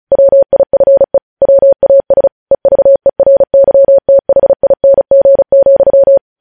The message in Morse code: WIFE WAS EVERYTHING,

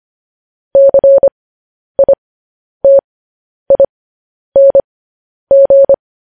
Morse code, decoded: CITING

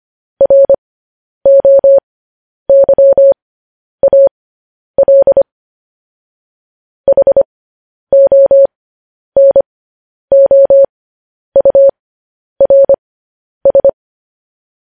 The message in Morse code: ROYAL HONOURS

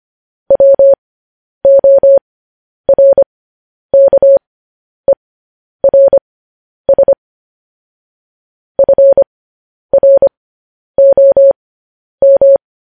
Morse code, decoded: WORKERS FROM

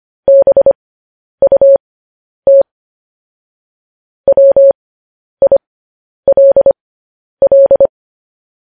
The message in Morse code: BUT WILL

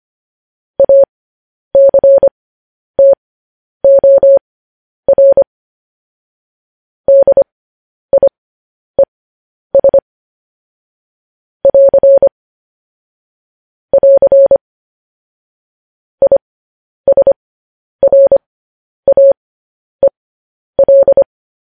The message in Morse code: ACTOR DIES <AR> <AR> ISRAEL

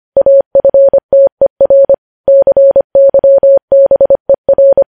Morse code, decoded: AFTER CYBER

